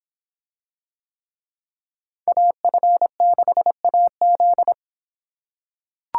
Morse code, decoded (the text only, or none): AF6AZ